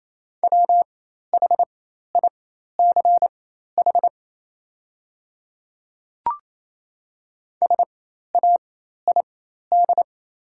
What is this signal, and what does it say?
Morse code, heard as a keyed tone, about 28 wpm, spaced as for 13 wpm, 700 Hz. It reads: WHICH SAID
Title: 4x Character Spacing - Top 100 Words - 28wpm